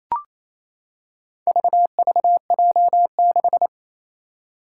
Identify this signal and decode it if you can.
Morse code, heard as a keyed tone, about 28 wpm, 700 Hz. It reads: VVJ6
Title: Four Challenging Letter-Number Combinations 28wpm